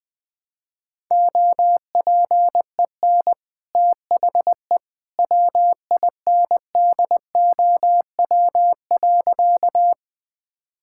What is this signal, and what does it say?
Morse code, heard as a keyed tone, about 20 wpm, 700 Hz. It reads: OPEN THE WINDOW.
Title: Easy Sentences with Hints - 20wpm